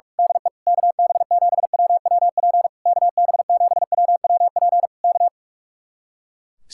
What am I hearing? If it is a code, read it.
Morse code, heard as a keyed tone, about 46 wpm, 700 Hz. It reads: DE KB7WWP KB7WWP K
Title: Speed-Racing Common QSO Phrases Encore - 45wpm